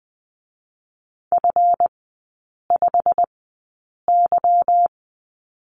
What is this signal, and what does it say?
Morse code, heard as a keyed tone, about 20 wpm, spaced as for 8 wpm, 690 Hz. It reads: F5Y
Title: 145 Lesson 24e - Letter y - Three Characters 20wpm